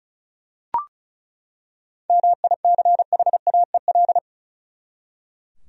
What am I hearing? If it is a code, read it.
Morse code, heard as a keyed tone, about 36 wpm, 700 Hz. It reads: MICHAEL
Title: USA Common Names - 35wpm